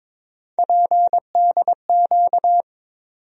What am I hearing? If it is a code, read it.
Morse code, heard as a keyed tone, about 21 wpm, 700 Hz. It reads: PDQ
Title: USA Call Sign Postfixes - 3 Letters 22wpm